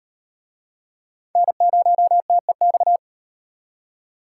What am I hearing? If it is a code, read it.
Morse code, heard as a keyed tone, about 35 wpm, 700 Hz. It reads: N0TEX